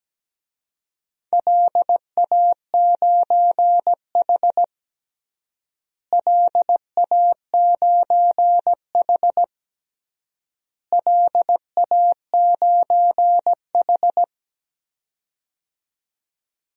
Morse code, heard as a keyed tone, about 17 wpm, 700 Hz. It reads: LA9H LA9H LA9H
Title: DXCC-100 Call Signs - 4 Characters - Repeated 3x - 17wpm